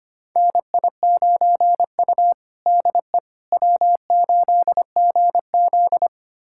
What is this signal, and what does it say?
Morse code, heard as a keyed tone, about 25 wpm, 700 Hz. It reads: NI9U DE W8GZ
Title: USA Callsign DE Callsign - 4 Characters - 25wpm